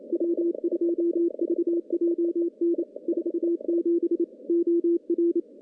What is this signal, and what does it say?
Morse code, heard as a keyed tone, about 27 wpm, 340 Hz. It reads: W2VJN 47 OR
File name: contests / CW-SS-k5tr- -0600